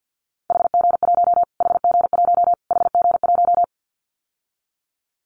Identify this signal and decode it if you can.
Morse code, heard as a keyed tone, about 49 wpm, 700 Hz. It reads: HG1 HG1 HG1
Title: Callsign Prefixes for Top 25 DXCC - Repeated 3x - 50wpm